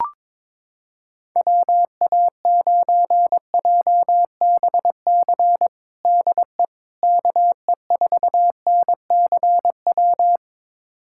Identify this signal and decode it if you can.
Morse code, heard as a keyed tone, about 22 wpm, 700 Hz. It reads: WA9JBC DE KE4NCW
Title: USA Callsign DE Callsign - 6 Characters - 22wpm